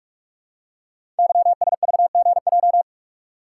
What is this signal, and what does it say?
Morse code, heard as a keyed tone, about 44 wpm, 700 Hz. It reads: YSVGJ
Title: Code Groups - 5 Letters - 45wpm